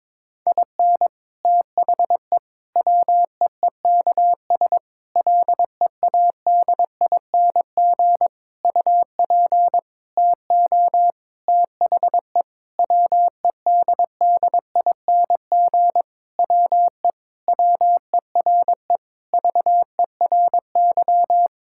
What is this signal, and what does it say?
Morse code, heard as a keyed tone, about 22 wpm, 700 Hz. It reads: IN THE WEEKS LEADING UP TO THE WEDDING WE WERE VERY
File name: Sentences with 7-Word Phrases - 22wpm